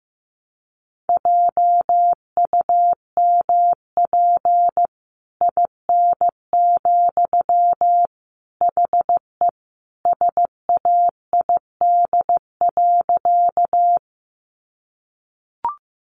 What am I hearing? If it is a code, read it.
Morse code, heard as a keyed tone, about 15 wpm, 700 Hz. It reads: JUMP IN, HE SAID.
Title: Sentences from Top 1300 Words 15wpm